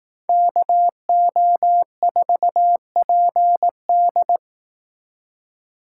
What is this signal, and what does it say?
Morse code, heard as a keyed tone, about 18 wpm, 700 Hz. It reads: KO4PD
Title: Speed Racing - USA 2x2 Call Signs 15wpm